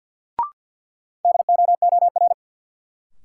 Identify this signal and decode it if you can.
Morse code, heard as a keyed tone, about 49 wpm, 700 Hz. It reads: DOOR
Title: Sets of 2 Words - New Word Drill - 50wpm